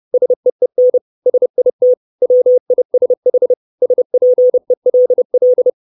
Morse code, heard as a keyed tone, about 30 wpm, 500 Hz. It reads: SEEN SIT WISH SPELL
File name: 632 Words 30WPM